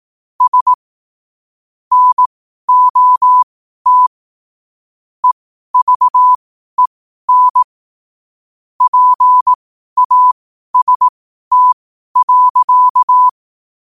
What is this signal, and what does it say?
Morse code, heard as a keyed tone, about 18 wpm, spaced as for 12 wpm, 1000 Hz. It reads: S NOT EVEN PAST.
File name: qotdfiles / 12wpm